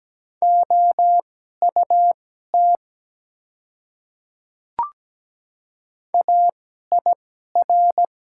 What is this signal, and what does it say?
Morse code, heard as a keyed tone, about 17 wpm, spaced as for 12 wpm, 700 Hz. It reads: OUT AIR
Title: 2x Character Spacing - Top 300 Words - 17wpm